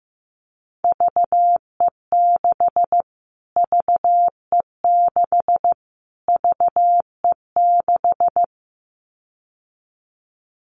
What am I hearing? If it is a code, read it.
Morse code, heard as a keyed tone, about 14 wpm, 700 Hz. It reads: VE6 VE6 VE6
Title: Callsign Prefixes for Top 25 DXCC - Repeated 3x - 15wpm